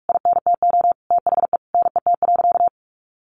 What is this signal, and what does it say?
Morse code, heard as a keyed tone, about 46 wpm, 700 Hz. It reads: INTO THE NET.
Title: Sentences from Top 1400 Words 45wpm